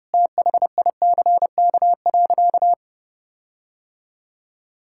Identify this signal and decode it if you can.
Morse code, heard as a keyed tone, about 30 wpm, 700 Hz. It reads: THICK.
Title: Sentences from Top 1600 Words 30wpm